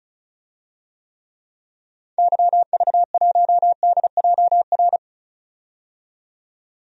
Morse code, heard as a keyed tone, about 36 wpm, 700 Hz. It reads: YV1DJR
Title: DXCC-100 Call Signs - 6 Characters - 35wpm